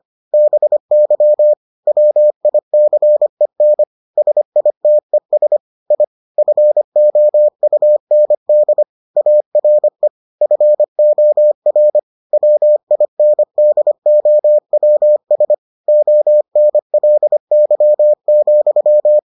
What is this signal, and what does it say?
Morse code, heard as a keyed tone, about 25 wpm, 600 Hz. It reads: BY WICEN SITES I FOUND ARE FOR WINDOWS ONLY,